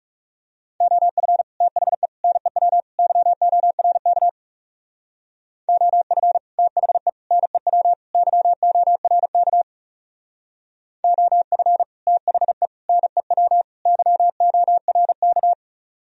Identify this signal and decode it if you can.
Morse code, heard as a keyed tone, about 38 wpm, 700 Hz. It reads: OF THE NEW YORK OF THE NEW YORK OF THE NEW YORK